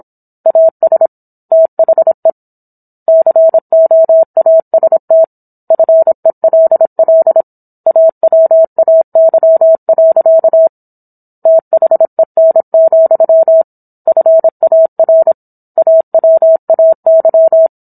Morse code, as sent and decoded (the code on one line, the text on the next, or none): .- ... / - .... / -.-. --- .- ... - / ..-. .-.. .-.. / .- .-- .- -.-- .-.-.- / - .... -. --..-- / ..-. .- .-. / .- .-- .- -.--
AS THE COAST FELL AWAY. THEN, FAR AWAY